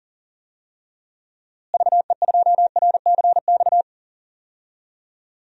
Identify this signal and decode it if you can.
Morse code, heard as a keyed tone, about 40 wpm, 700 Hz. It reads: VE2RCX
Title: DXCC-100 Call Signs - 6 Characters - Repeated 3x - 40wpm